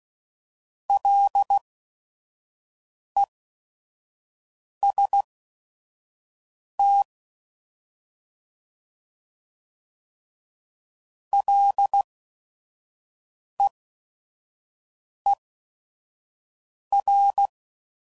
Morse code, heard as a keyed tone, about 16 wpm, spaced as for 5 wpm, 800 Hz. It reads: LEST LEER
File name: Prefixes / By First Letter / Ls